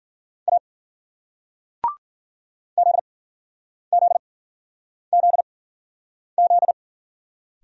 Morse code, heard as a keyed tone, about 53 wpm, spaced as for 9 wpm, 700 Hz. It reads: A ZZZZ